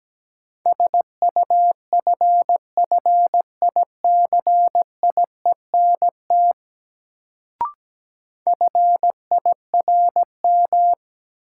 Morse code, heard as a keyed tone, about 17 wpm, 700 Hz. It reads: SUFFICIENT FIRM
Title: Sentences from Top 1400 Words - New Word Drill - 17wpm